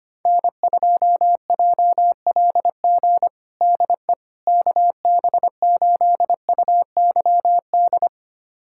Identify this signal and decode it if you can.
Morse code, heard as a keyed tone, about 25 wpm, 700 Hz. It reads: N2JLG DE KB8UYD
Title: USA Callsign DE Callsign 25wpm